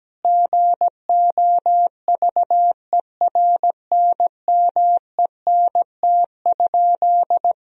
Morse code, heard as a keyed tone, about 17 wpm, 700 Hz. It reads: GOVERNMENT?
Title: Sentences from Top 1000 Words - Review Part 4 - 17wpm